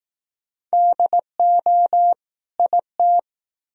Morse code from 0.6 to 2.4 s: DO

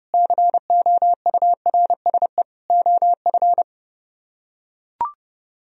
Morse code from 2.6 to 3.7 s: OF

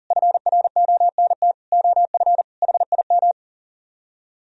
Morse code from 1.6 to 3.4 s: OF HIM

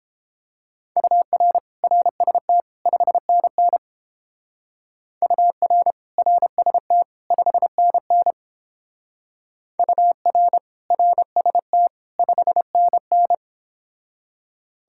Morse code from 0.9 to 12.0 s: UR RST 5NN UR RST 5NN UR RST

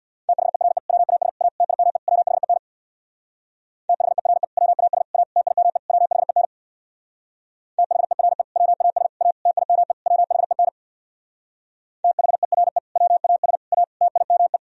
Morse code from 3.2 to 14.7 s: THERE WAS A TIME WHEN THERE WAS A TIME WHEN THERE WAS A TIME

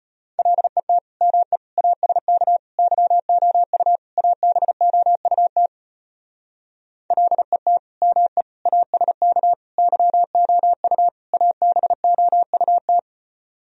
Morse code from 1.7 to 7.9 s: ASK YOU ABOUT LET

